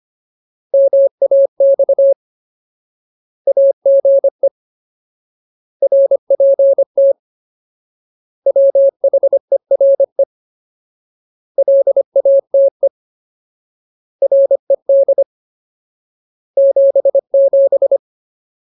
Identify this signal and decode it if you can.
Morse code, heard as a keyed tone, about 25 wpm, 550 Hz. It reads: MAX AGE RPT WHERE LATE RED 77
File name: Ultimate Word List / 25WPM Ultimate